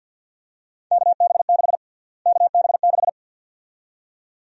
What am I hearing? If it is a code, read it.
Morse code, heard as a keyed tone, about 52 wpm, 700 Hz. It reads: KB6 KB6
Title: USA Call Sign Prefixes - Repeated 2x - 50wpm